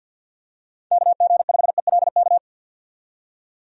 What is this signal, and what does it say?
Morse code, heard as a keyed tone, about 52 wpm, 700 Hz. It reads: KG5ELK
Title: USA Call Signs - 6 Characters - 50wpm